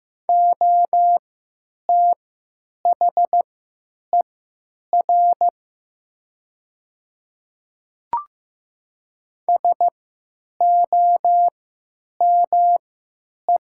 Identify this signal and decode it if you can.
Morse code, heard as a keyed tone, about 15 wpm, spaced as for 9 wpm, 700 Hz. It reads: OTHER SOME
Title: 3x Character Spacing - Top 100 Words - 15wpm